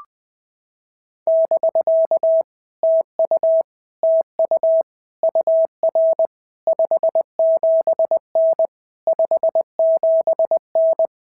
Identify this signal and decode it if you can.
Morse code, heard as a keyed tone, about 20 wpm, 650 Hz. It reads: <BK> TU TU UR 57N 57N
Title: POTA QSO Practice - 20wpm